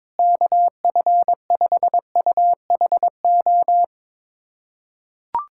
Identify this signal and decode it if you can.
Morse code, heard as a keyed tone, about 22 wpm, 700 Hz. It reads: KF5UHO
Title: USA Callsign DE Callsign - 6 Characters - 22wpm